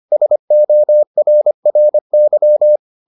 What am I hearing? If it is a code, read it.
Morse code, heard as a keyed tone, about 25 wpm, 600 Hz. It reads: SORRY